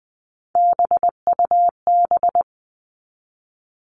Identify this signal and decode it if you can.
Morse code, heard as a keyed tone, about 20 wpm, 700 Hz. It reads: BUB
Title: All Cut Numbers - 3-Digit 20wpm